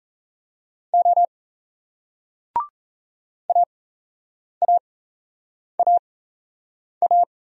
Morse code, heard as a keyed tone, about 38 wpm, spaced as for 8 wpm, 700 Hz. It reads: O AAAA